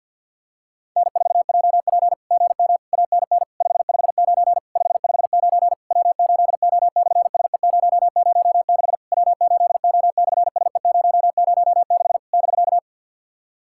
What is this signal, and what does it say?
Morse code, heard as a keyed tone, about 50 wpm, 700 Hz. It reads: N4JP GM ANN 559 559 W8O/SE006 W8O/SE006 <BK>
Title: SOTA QSO Practice 50wpm